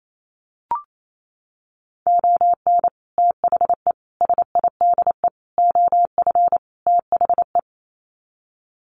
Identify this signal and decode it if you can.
Morse code, heard as a keyed tone, about 28 wpm, 700 Hz. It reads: ON THE SIDE OF THE
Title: Sets of 5 Words 28wpm